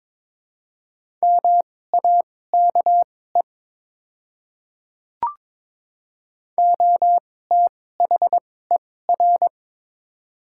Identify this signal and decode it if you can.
Morse code, heard as a keyed tone, about 22 wpm, spaced as for 16 wpm, 700 Hz. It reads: MAKE OTHER
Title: 2x Character Spacing - Top 100 Words - 22wpm